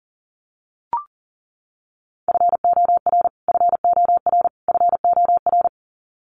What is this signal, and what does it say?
Morse code, heard as a keyed tone, about 40 wpm, 700 Hz. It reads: FOR FOR FOR
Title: Top 300 Words Repeated 40wpm